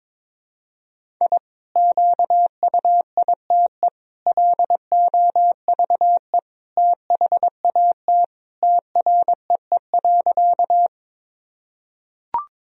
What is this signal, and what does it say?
Morse code, heard as a keyed tone, about 22 wpm, 700 Hz. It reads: I QUITE LOVE THAT TREE.
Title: Sentences from Top 1000 Words - Review Part 1 - 22wpm